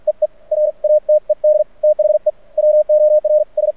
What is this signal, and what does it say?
Morse code, heard as a keyed tone, about 46 wpm, 610 Hz. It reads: EE RATED THE WORS
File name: archived / CRSnet